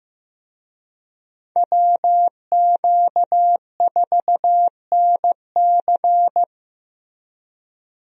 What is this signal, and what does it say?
Morse code, heard as a keyed tone, about 15 wpm, 700 Hz. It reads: WQ4NC
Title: USA Call Signs - 5 Characters - Repeated 3x - 15wpm